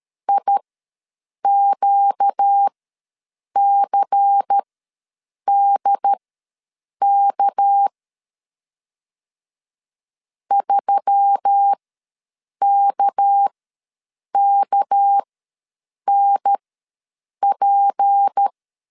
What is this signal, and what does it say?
Morse code, heard as a keyed tone, about 13 wpm, spaced as for 7 wpm, 800 Hz. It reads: IQCDK 3KKNP
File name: LEZIONI / 6 PKX0 / 6 PK ws 12wpm 3116